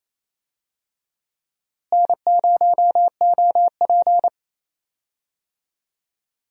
Morse code, heard as a keyed tone, about 28 wpm, 700 Hz. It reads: N0OP
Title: USA Call Signs - 4 Characters - Repeated 2x - 28wpm